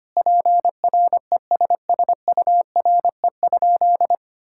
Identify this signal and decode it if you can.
Morse code, heard as a keyed tone, about 25 wpm, 700 Hz. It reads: PRESSURE?
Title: Sentences from Top 1000 Words - Review Part 6 - 25wpm